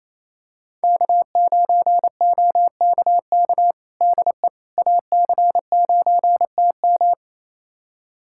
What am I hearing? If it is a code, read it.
Morse code, heard as a keyed tone, about 28 wpm, 700 Hz. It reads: K9OKK DE AC9TM
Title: USA Callsign DE Callsign - 5 Characters - 28wpm